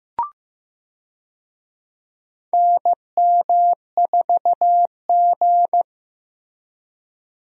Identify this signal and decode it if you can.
Morse code, heard as a keyed tone, about 15 wpm, 700 Hz. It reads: NM4G
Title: USA Call Signs - 4 Characters - Repeated 2x - 15wpm